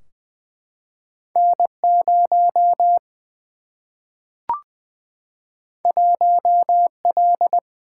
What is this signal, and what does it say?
Morse code, heard as a keyed tone, about 20 wpm, 700 Hz. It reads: N 0 1 L
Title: Single-Letter Single-Number 20wpm